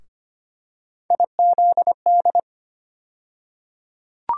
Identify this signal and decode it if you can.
Morse code, heard as a keyed tone, about 25 wpm, 700 Hz. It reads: IZD
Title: USA Call Sign Postfixes - 3 Letters 25wpm